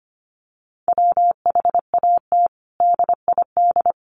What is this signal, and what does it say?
Morse code, heard as a keyed tone, about 25 wpm, 700 Hz. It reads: WHAT DID
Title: Sentences from Top 1400 Words 25wpm